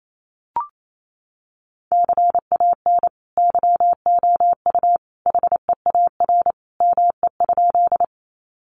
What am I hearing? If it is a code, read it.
Morse code, heard as a keyed tone, about 28 wpm, 700 Hz. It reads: CAN YOU HEAR ME?
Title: Sentences from Top 300 Words 28wpm